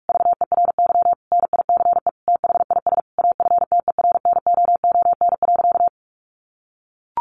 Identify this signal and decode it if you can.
Morse code, heard as a keyed tone, about 44 wpm, 700 Hz. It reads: VERY NICE THIS AFTERNOON.
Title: Sentences from Top 1500 Words 45wpm